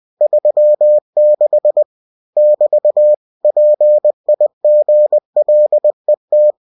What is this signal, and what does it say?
Morse code, heard as a keyed tone, about 20 wpm, 600 Hz. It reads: 36 <BT> PIGLET
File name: cw-books / winnie-the-pooh / winnie-20wpm-page-0036